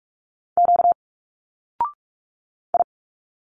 Morse code, heard as a keyed tone, about 46 wpm, 700 Hz. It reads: Q I
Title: Single Letters 45wpm